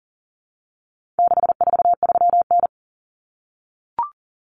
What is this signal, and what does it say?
Morse code, heard as a keyed tone, about 40 wpm, 700 Hz. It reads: 643N
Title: Common Cut Numbers - 4-Digit 40wpm